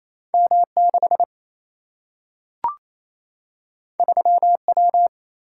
Morse code, heard as a keyed tone, about 28 wpm, 700 Hz. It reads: M 6 3 W
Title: Single-Letter Single-Number 28wpm